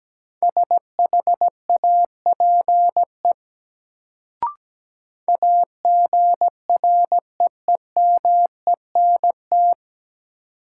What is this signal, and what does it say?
Morse code, heard as a keyed tone, about 17 wpm, 700 Hz. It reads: SHAPE AGREEMENT
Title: Sets of 5 Words Encore - New Word Drill - 17wpm